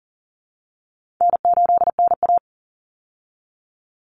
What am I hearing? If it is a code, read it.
Morse code, heard as a keyed tone, about 39 wpm, 700 Hz. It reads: N8NA
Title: Easy Call Signs Repeated 2x - 40wpm